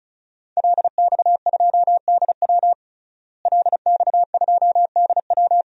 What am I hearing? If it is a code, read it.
Morse code, heard as a keyed tone, about 36 wpm, 700 Hz. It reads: LX2DW LX2DW
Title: DXCC-100 Call Signs - 5 Characters - Repeated 2x - 35wpm